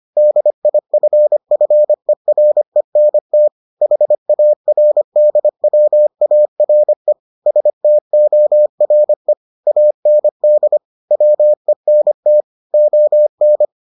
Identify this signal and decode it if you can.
Morse code, heard as a keyed tone, about 25 wpm, 600 Hz. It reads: DIFFERENT HARDWARE STORE AND WENT ON